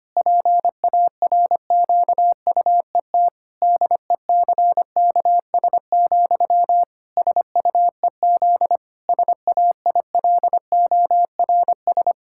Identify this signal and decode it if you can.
Morse code, heard as a keyed tone, about 25 wpm, 700 Hz. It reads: PARQUET DECKS, SUEZ SAILORS
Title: Pangram Challenge v2 - 25wpm